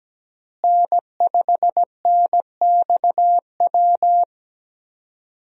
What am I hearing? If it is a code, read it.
Morse code, heard as a keyed tone, about 17 wpm, 700 Hz. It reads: N5NXW
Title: USA Call Signs - 5 Characters - Repeated 2x - 17wpm